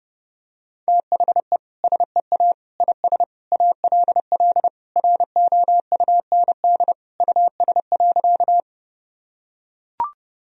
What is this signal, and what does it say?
Morse code, heard as a keyed tone, about 32 wpm, 700 Hz. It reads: THE SEA IS ALL ROUND US.
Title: Sentences from Top 1000 Words - Review Part 1 - 30wpm